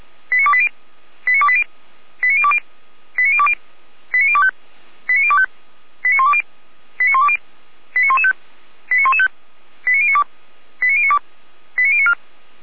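Five-tone selective call (ZVEI), heard as quick runs of five tones, tones of about 70 ms. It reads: 89280 89280 89020 89020 89025 89025 89120 89120 89115 89115 89092 89092 89094
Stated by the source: ZVEI five-tone selective call received with an SDR, from the audio server of the Orlamünde fire brigade (feuerwehr-orlamuende).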